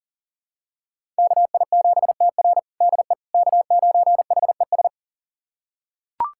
Morse code, heard as a keyed tone, about 37 wpm, 700 Hz. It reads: KI7TR DE K9HES
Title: USA Callsign DE Callsign - 5 Characters - 40wpm